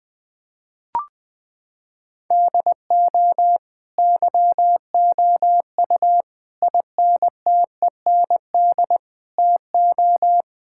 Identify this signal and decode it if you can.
Morse code, heard as a keyed tone, about 20 wpm, 700 Hz. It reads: DO YOU INTEND TO
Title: Sentences from Top 1600 Words 20wpm